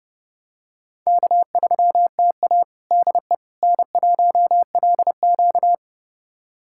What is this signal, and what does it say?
Morse code, heard as a keyed tone, about 29 wpm, 700 Hz. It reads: K3TA DE N1LQ
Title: USA Callsign DE Callsign - 4 Characters - 30wpm